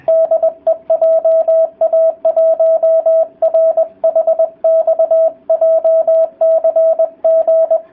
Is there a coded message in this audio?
Yes: Morse code from 0.1 to 7.8 s, heard as a keyed tone, about 20 wpm, 650 Hz, with radio noise. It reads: DE JA1RHX JCG